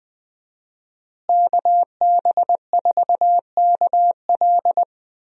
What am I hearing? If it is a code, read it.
Morse code, heard as a keyed tone, about 20 wpm, 700 Hz. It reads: KB4KL
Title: USA Call Signs - 5 Characters - Repeated 3x - 20wpm